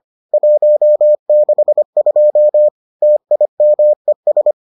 Morse code, heard as a keyed tone, about 25 wpm, 600 Hz. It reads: 162 TIMES